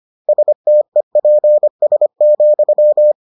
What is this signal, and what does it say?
Morse code, heard as a keyed tone, about 25 wpm, 600 Hz. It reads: STEPS,